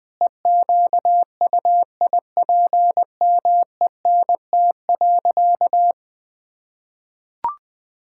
Morse code, heard as a keyed tone, about 20 wpm, 700 Hz. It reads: EQUIPMENT.
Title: Sentences from Top 1000 Words - Review Part 6 - 20wpm